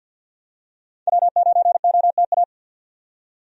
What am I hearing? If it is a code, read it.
Morse code, heard as a keyed tone, about 49 wpm, 700 Hz. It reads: W9OTA
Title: USA Call Signs - 5 Characters - 50wpm